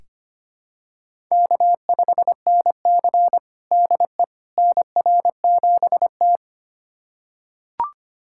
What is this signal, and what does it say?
Morse code, heard as a keyed tone, about 25 wpm, 700 Hz. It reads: K5NC DE NR7T
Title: USA Callsign DE Callsign - 4 Characters - 25wpm